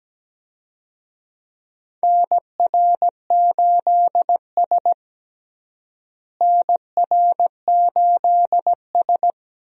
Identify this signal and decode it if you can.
Morse code, heard as a keyed tone, about 17 wpm, 700 Hz. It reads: NR8S NR8S
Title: USA Call Signs - 4 Characters - Repeated 2x - 17wpm